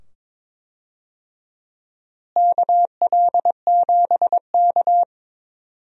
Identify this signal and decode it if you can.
Morse code, heard as a keyed tone, about 22 wpm, 700 Hz. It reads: KL7K
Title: DXCC-100 Call Signs - 4 Characters - Repeated 2x - 22wpm